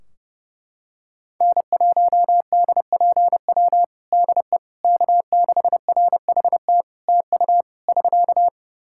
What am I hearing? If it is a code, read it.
Morse code, heard as a keyed tone, about 30 wpm, 700 Hz. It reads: N1DPW DE K6RHT TU <SK>
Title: Practice QSO - Exchange 4 - Ending - 30wpm